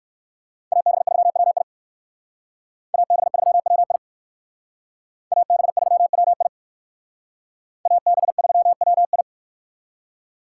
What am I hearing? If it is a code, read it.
Morse code, heard as a keyed tone, about 63 wpm, 700 Hz. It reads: AB3WI AB3WI AB3WI AB3WI